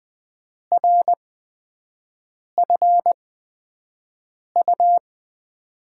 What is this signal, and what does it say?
Morse code, heard as a keyed tone, about 20 wpm, spaced as for 5 wpm, 700 Hz. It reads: RFU
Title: USA Call Signs - 5 Characters - 8x Character Spacing - 20wpm